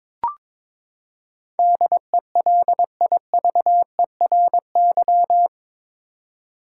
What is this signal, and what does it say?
Morse code, heard as a keyed tone, about 22 wpm, 700 Hz. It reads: DELIVERY
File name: practice / 22wpm / Eight Letter Words - 22wpm